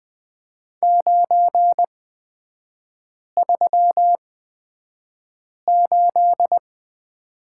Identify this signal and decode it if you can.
Morse code, heard as a keyed tone, about 20 wpm, spaced as for 5 wpm, 700 Hz. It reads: 938
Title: 255 Lesson 41e - Period - Three Characters 20wpm